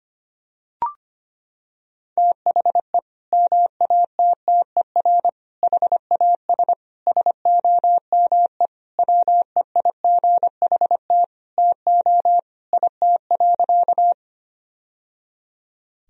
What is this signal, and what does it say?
Morse code, heard as a keyed tone, about 25 wpm, 700 Hz. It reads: THE MATTER HAS SOME WEIGHT TO IT.
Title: Sentences from Top 1000 Words - Review Part 3 - 25wpm